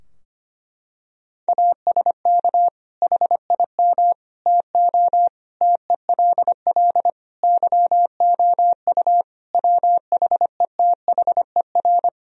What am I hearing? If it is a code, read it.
Morse code, heard as a keyed tone, about 25 wpm, 700 Hz. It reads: ASK HIM TO TELL YOU WHETHER
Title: Sentences from Top 1000 Words - Review Part 3 - 25wpm